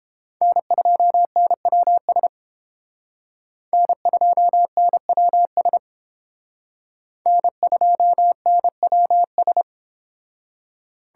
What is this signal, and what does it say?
Morse code, heard as a keyed tone, about 30 wpm, 700 Hz. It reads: N2NWS N2NWS N2NWS